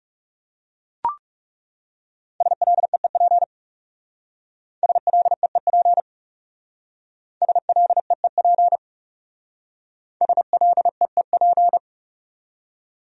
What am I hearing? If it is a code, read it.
Morse code, heard as a keyed tone, about 38 wpm, 700 Hz. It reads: SLEEP SLEEP SLEEP SLEEP